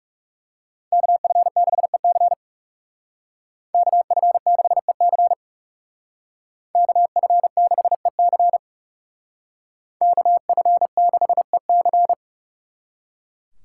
Morse code, heard as a keyed tone, about 38 wpm, 700 Hz. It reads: KF6EC KF6EC KF6EC KF6EC